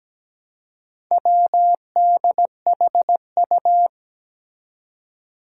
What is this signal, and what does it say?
Morse code, heard as a keyed tone, about 17 wpm, 700 Hz. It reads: WDHU
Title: Four Challenging Letter-Number Combinations 17wpm